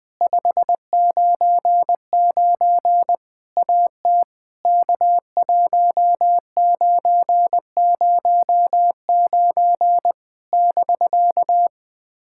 Morse code, heard as a keyed tone, about 20 wpm, 700 Hz. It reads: 599 AT K1909 <BK>
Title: POTA QSO Practice - 20wpm